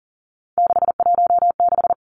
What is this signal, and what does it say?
Morse code, heard as a keyed tone, about 42 wpm, 700 Hz. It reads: 616
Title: Three Challenging Letter-Number Combinations 40wpm